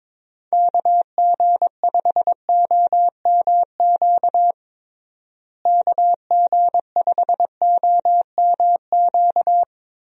Morse code, heard as a keyed tone, about 22 wpm, 700 Hz. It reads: KG5OMQ KG5OMQ
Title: USA Call Signs - 6 Characters - Repeated 2x - 22wpm